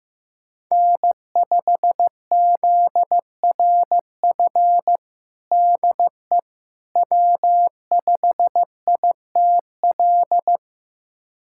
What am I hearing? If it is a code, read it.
Morse code, heard as a keyed tone, about 15 wpm, 700 Hz. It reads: N5ZRF DE W5ITL
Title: USA Callsign DE Callsign - 5 Characters - 15wpm